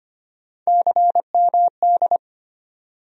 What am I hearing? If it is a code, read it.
Morse code, heard as a keyed tone, about 25 wpm, 700 Hz. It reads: CMD
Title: USA Call Sign Postfixes - 3 Letters 25wpm